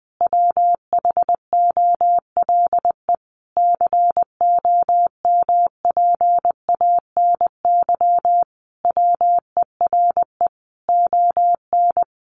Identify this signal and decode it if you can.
Morse code, heard as a keyed tone, about 20 wpm, 700 Hz. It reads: WHOLE COMPANY WERE ON
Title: Sentences from Top 1000 Words 20wpm